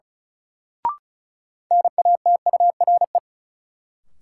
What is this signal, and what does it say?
Morse code, heard as a keyed tone, about 36 wpm, 700 Hz. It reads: NATURE
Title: Sentences from Top 600 Words - New Word Drill - 35wpm